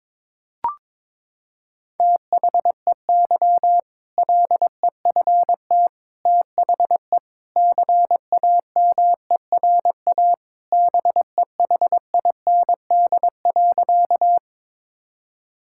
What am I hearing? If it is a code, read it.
Morse code, heard as a keyed tone, about 21 wpm, 700 Hz. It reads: THEY LEFT THE CAMERA BEHIND.
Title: Sentences from Top 1000 Words - Review Part 5 - 22wpm